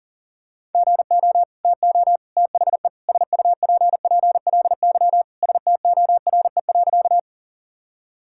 Morse code, heard as a keyed tone, about 37 wpm, 700 Hz. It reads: GO TO THE SUPPLY STORE.
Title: Sentences from Top 1000 Words - Review Part 5 - 40wpm